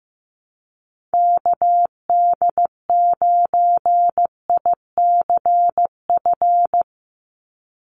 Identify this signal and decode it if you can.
Morse code, heard as a keyed tone, about 15 wpm, 700 Hz. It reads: KD9ICF
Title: USA Call Signs 15wpm